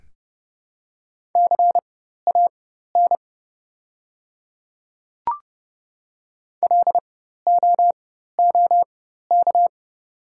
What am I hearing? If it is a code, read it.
Morse code, heard as a keyed tone, about 31 wpm, spaced as for 14 wpm, 700 Hz. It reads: CAN LOOK
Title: 4x Character Spacing - Top 100 Words - 30wpm